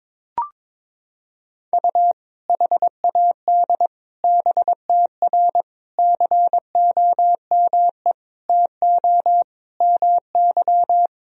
Morse code, heard as a keyed tone, about 22 wpm, 700 Hz. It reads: U HAD BTR COME TO MY
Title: Sentences from Top 400 Words and Abbreviations 22wpm